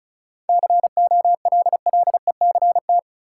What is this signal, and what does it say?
Morse code, heard as a keyed tone, about 36 wpm, 700 Hz. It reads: COLLECT
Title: Sentences from Top 1600 Words - New Word Drill - 35wpm